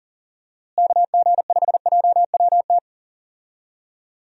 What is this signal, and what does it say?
Morse code, heard as a keyed tone, about 40 wpm, 700 Hz. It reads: KG5JWT
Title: USA Call Signs - 6 Characters - Repeated 3x - 40wpm